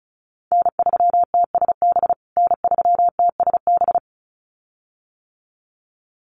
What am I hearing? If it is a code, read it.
Morse code, heard as a keyed tone, about 34 wpm, 700 Hz. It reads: N3TSB N3TSB
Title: USA Call Signs Repeated 2x 35wpm